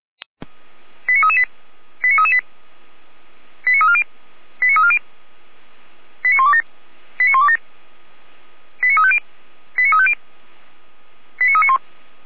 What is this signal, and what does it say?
Five-tone selective call (ZVEI), heard as quick runs of five tones, tones of about 70 ms. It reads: 89338 89338 89340 89340 89127 89127 89350 89350 89391